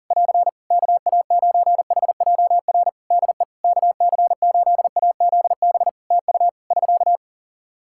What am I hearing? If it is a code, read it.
Morse code, heard as a keyed tone, about 40 wpm, 700 Hz. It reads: <AR> KA9HJR DE KC8AZB TU <SK>
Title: Practice QSO - Exchange 4 - Ending - 40wpm